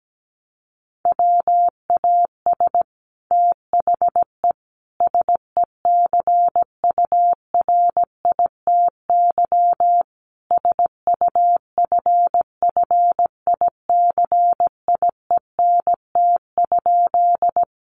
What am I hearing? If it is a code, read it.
Morse code, heard as a keyed tone, about 17 wpm, 700 Hz. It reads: WAS THE SECURITY SUFFICIENT?
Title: Sentences from Top 1400 Words 17wpm